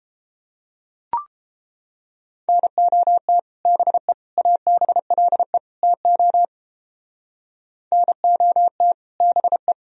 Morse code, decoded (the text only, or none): NOT BE ABLE TO NOT BE